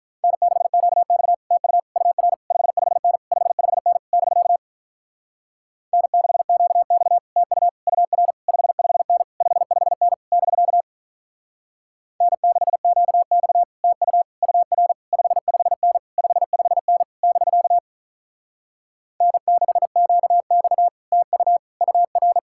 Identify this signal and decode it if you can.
Morse code, heard as a keyed tone, about 46 wpm, 700 Hz. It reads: N6QX TU UR 55N 55N <BK> N6QX TU UR 55N 55N <BK> N6QX TU UR 55N 55N <BK> N6QX TU UR